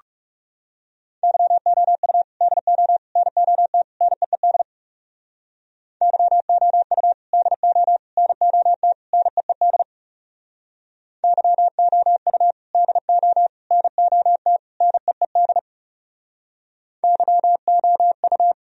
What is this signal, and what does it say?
Morse code, heard as a keyed tone, about 40 wpm, 700 Hz. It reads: YOU DO NOT NEED YOU DO NOT NEED YOU DO NOT NEED YOU